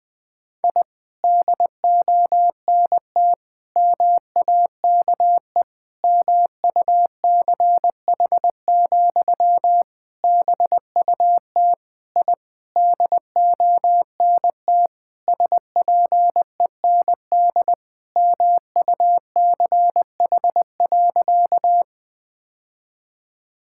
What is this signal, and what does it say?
Morse code, heard as a keyed tone, about 20 wpm, 700 Hz. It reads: I DONT MAKE MUCH, BUT I DONT SPEND MUCH.
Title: Sentences from Top 1000 Words - Review Part 5 - 20wpm